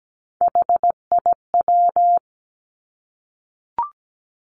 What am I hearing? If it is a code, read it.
Morse code, heard as a keyed tone, about 17 wpm, 700 Hz. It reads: HIW
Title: Three Challenging Letter-Number Combinations 17wpm